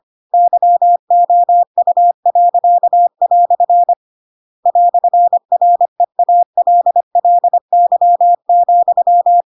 Morse code, heard as a keyed tone, about 25 wpm, 700 Hz. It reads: YOU." "REALLY,
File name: Pinocchio / Pinocchio ch7 25 25 ik1hge